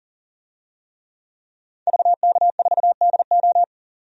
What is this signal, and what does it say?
Morse code, heard as a keyed tone, about 40 wpm, 700 Hz. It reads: VK4DO
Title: DXCC-100 Call Signs - 5 Characters - 40wpm